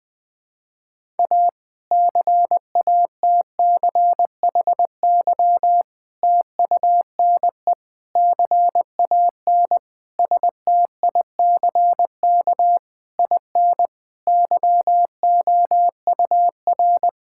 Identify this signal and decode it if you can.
Morse code, heard as a keyed tone, about 21 wpm, 700 Hz. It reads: A CATCHY TUNE CAN STICK IN YOUR
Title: Difficult Sentences with Hints - 20wpm